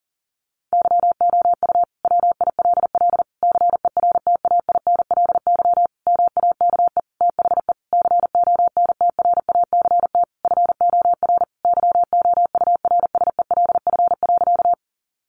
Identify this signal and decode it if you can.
Morse code, heard as a keyed tone, about 40 wpm, 700 Hz. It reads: YOU WILL CERTAINLY MAKE THE CONTRACT FOR YOURSELF.
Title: Sentences from Top 1400 Words 40wpm